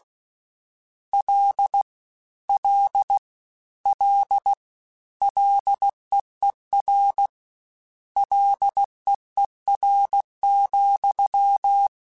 Morse code, heard as a keyed tone, about 16 wpm, 800 Hz. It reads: L L L LEER LEER,
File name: Combinations / Combinations-mp3s / Combinations@15